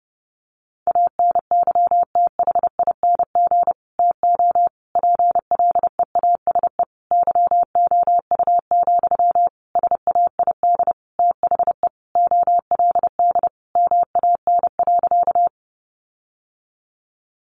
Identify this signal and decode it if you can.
Morse code, heard as a keyed tone, about 30 wpm, 700 Hz. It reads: ANYTHING TO PLEASE YOU, SAID THE OLD MAN.
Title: Sentences from Top 1500 Words 30wpm